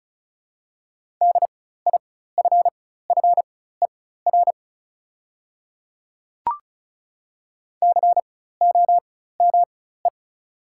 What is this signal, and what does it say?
Morse code, heard as a keyed tone, about 36 wpm, spaced as for 16 wpm, 700 Hz. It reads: DIFFER COME
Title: 4x Character Spacing - Top 200 Words - 35wpm